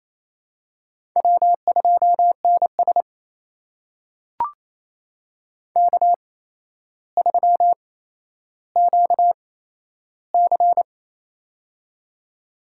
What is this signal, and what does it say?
Morse code, heard as a keyed tone, about 28 wpm, spaced as for 12 wpm, 700 Hz. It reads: W2NS K3QC
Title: USA Call Signs - 8x Character Spacing - 28wpm